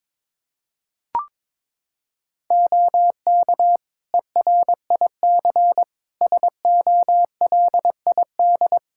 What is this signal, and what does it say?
Morse code, heard as a keyed tone, about 22 wpm, 700 Hz. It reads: OK ERIC SOLID
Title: Practice QSO - Exchange 3 - More Info - 22wpm